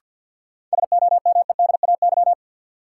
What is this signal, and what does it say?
Morse code, heard as a keyed tone, about 51 wpm, 700 Hz. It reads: SOMEDAY
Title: Sentences from Top 1600 Words - New Word Drill Alt - 50wpm